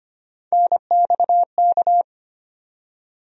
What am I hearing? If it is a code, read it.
Morse code, heard as a keyed tone, about 25 wpm, 700 Hz. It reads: NXK
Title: USA Call Sign Postfixes - 3 Letters 25wpm